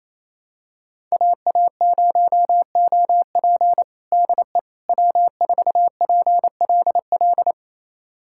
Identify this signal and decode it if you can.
Morse code, heard as a keyed tone, about 28 wpm, 690 Hz. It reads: AA0OP DE W4PLL
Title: USA Callsign DE Callsign - 5 Characters - 28wpm